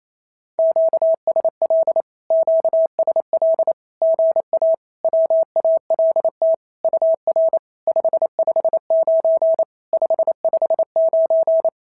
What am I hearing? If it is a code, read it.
Morse code, heard as a keyed tone, about 28 wpm, 650 Hz. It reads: QSL QSL GA WALT UR 559 559
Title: SOTA QSO Practice 28wpm